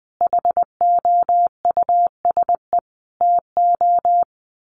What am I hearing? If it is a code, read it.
Morse code, heard as a keyed tone, about 20 wpm, 700 Hz. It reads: HOUSE TO
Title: Sentences from Top 1200 Words 20wpm